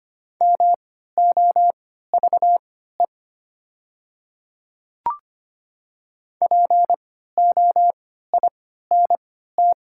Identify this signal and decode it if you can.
Morse code, heard as a keyed tone, about 25 wpm, spaced as for 14 wpm, 700 Hz. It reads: MOVE POINT
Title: 3x Character Spacing - Top 500 Words - 25wpm